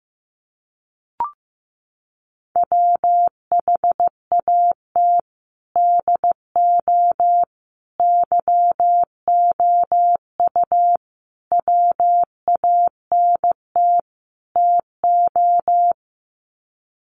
Morse code, heard as a keyed tone, about 15 wpm, 700 Hz. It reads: WHAT DO YOU WANT TO
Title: Sets of 5 Words 15wpm